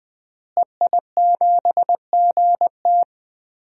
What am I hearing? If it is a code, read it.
Morse code, heard as a keyed tone, about 20 wpm, 700 Hz. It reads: EI7GT
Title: DXCC-100 Call Signs - 5 Characters - Repeated 3x - 20wpm